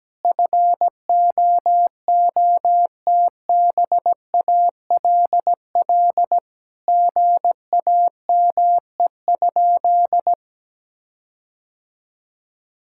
Morse code, heard as a keyed tone, about 17 wpm, 700 Hz. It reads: FOOTBALL GAME?
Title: Sentences from Top 1600 Words 17wpm